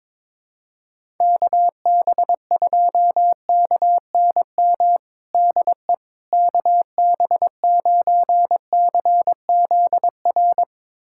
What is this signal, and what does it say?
Morse code, heard as a keyed tone, about 22 wpm, 700 Hz. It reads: KB2KNM DE KB9CZR
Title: USA Callsign DE Callsign - 6 Characters - 22wpm